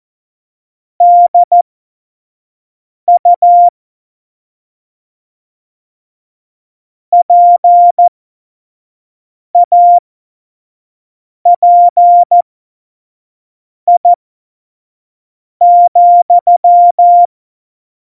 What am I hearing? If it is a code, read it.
Morse code, heard as a keyed tone, about 14 wpm, spaced as for 5 wpm, 700 Hz. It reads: DU PAPI,
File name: Morsekurs / Witze / Witz 07. November 2017, 25 F70